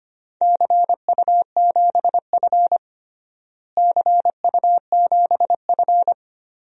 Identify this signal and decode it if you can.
Morse code, heard as a keyed tone, about 25 wpm, 700 Hz. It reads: CU7F CU7F
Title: DXCC-100 Call Signs - 4 Characters - Repeated 2x - 25wpm